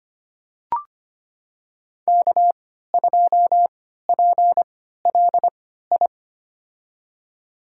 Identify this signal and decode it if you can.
Morse code, heard as a keyed tone, about 25 wpm, spaced as for 14 wpm, 700 Hz. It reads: K2PLI
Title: USA Call Signs - 3x Character Spacing - 25wpm